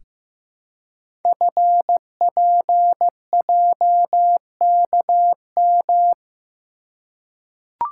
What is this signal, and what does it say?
Morse code, heard as a keyed tone, about 15 wpm, 700 Hz. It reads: FPJKM
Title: Code Groups - 5 Letter:Numbers - 15wpm